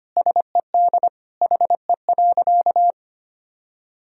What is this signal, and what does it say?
Morse code, heard as a keyed tone, about 25 wpm, 700 Hz. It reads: SED HE.
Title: Sentences from Top 400 Words and Abbreviations 25wpm